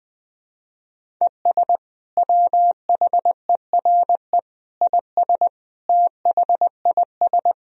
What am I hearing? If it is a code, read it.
Morse code, heard as a keyed tone, about 20 wpm, 700 Hz. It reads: ES WHERE IS THIS
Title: Sentences from Top 200 Words and Abbreviations 20wpm